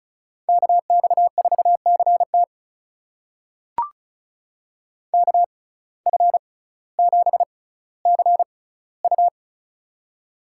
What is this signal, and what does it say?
Morse code, heard as a keyed tone, about 36 wpm, spaced as for 18 wpm, 700 Hz. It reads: KX4CT KF7CU
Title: USA Call Signs - 5 Characters - 6x Character Spacing - 35wpm